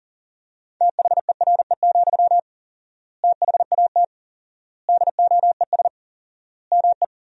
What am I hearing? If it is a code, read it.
Morse code, heard as a keyed tone, about 40 wpm, 700 Hz. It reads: THERE, THAT DOES ME
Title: 4x Word-Spacing - Sentences from Top 200 Words - 40wpm